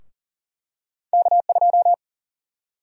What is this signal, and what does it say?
Morse code, heard as a keyed tone, about 40 wpm, 700 Hz. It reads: K2
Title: USA Call Sign Prefixes - Repeated 2x - 40wpm